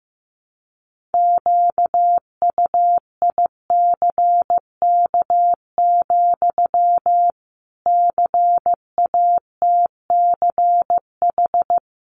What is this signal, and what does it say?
Morse code, heard as a keyed tone, about 15 wpm, 700 Hz. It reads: QUICK, CATCH
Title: Sentences from Top 1500 Words 15wpm